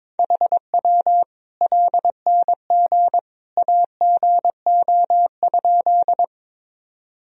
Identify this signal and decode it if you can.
Morse code, heard as a keyed tone, about 22 wpm, 700 Hz. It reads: HW LNG AGO?
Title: Sentences from Top 500 Words and Abbreviations 22wpm